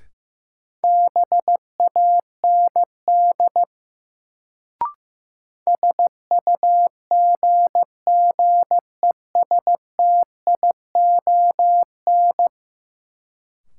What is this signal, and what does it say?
Morse code, heard as a keyed tone, about 15 wpm, 700 Hz. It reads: BAND SUGGESTION
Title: Sentences from Top 1600 Words - New Word Drill - 15wpm